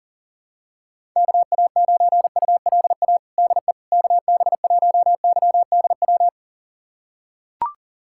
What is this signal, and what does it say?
Morse code, heard as a keyed tone, about 40 wpm, 700 Hz. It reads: KA9ULA DE KB1YDW
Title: USA Callsign DE Callsign - 6 Characters - 40wpm